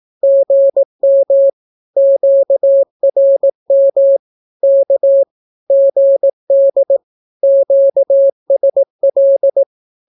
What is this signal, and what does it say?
Morse code, heard as a keyed tone, about 18 wpm, 550 Hz. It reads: GM QRM K GD QSL
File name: naukapodstaw / skroty / skroty